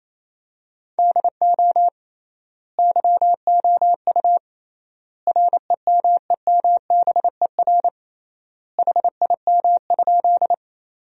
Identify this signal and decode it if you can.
Morse code, heard as a keyed tone, about 28 wpm, 700 Hz. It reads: DO YOU REMEMBER HIM?
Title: 3x Word-Spacing - Sentences from Top 400 Words - 28wpm